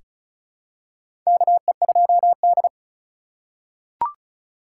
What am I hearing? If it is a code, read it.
Morse code, heard as a keyed tone, about 36 wpm, 700 Hz. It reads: KE2D
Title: K1USN SST QSO Practice - 35wpm